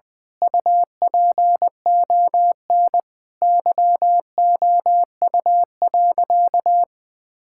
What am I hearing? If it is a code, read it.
Morse code, heard as a keyed tone, about 20 wpm, 700 Hz. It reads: UPON YOU.
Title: Sentences from Top 1600 Words 20wpm